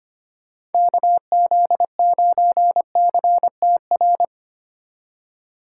Morse code, heard as a keyed tone, about 25 wpm, 700 Hz. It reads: KZ9CTR
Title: USA Call Signs - 6 Characters - 3x Character Spacing - 25wpm